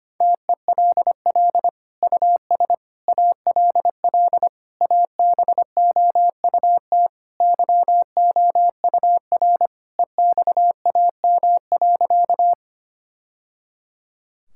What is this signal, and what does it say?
Morse code, heard as a keyed tone, about 25 wpm, 700 Hz. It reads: TELL US ALL ABOUT YOUR EXAM.
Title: Sentences from Top 1000 Words - Review Part 5 - 25wpm